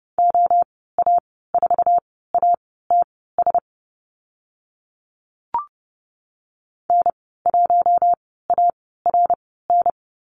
Morse code, heard as a keyed tone, about 30 wpm, spaced as for 17 wpm, 700 Hz. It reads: OA4ATS N1ARN
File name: Lesson 14g - Number 2 - Call Signs 30wpm